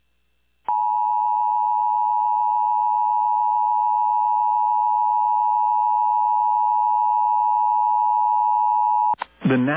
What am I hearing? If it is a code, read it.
Emergency Alert System (SAME) tones (the 853 and 960 Hz attention tone), with speech. It is heard from 0.6 to 9.2 s.